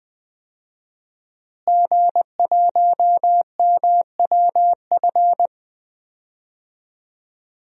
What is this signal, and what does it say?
Morse code, heard as a keyed tone, about 20 wpm, 700 Hz. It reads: G1MWF
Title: DXCC-100 Call Signs - 5 Characters - Repeated 3x - 20wpm